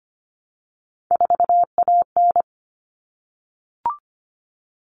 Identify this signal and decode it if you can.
Morse code, heard as a keyed tone, about 25 wpm, 700 Hz. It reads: 4AN
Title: All Cut Numbers - 3-Digit 25wpm